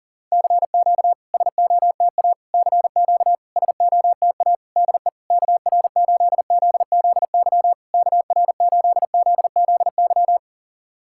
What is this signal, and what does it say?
Morse code, heard as a keyed tone, about 41 wpm, 700 Hz. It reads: CQ SOTA CQ SOTA DE KR8ZZY KR8ZZY
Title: SOTA QSO Practice 40wpm